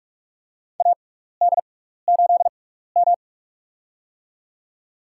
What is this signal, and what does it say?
Morse code, heard as a keyed tone, about 46 wpm, spaced as for 16 wpm, 700 Hz. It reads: AD8M
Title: USA Call Signs - 4 Characters - 6x Character Spacing - 45wpm